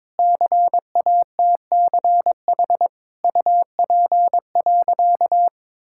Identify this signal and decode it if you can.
Morse code, heard as a keyed tone, about 22 wpm, 700 Hz. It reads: CATCH UP.
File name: Difficult Sentences with Hints - 22wpm